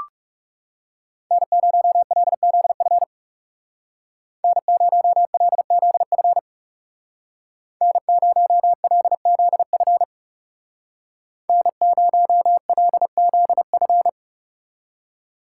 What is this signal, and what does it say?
Morse code, heard as a keyed tone, about 35 wpm, 700 Hz. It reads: N0LZF N0LZF N0LZF N0LZF
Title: Speed Racing - USA 1x3 Call Signs 30wpm